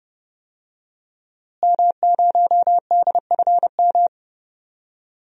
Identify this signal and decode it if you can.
Morse code, heard as a keyed tone, about 30 wpm, 700 Hz. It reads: M0DFM
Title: DXCC-100 Call Signs - 5 Characters - Repeated 3x - 30wpm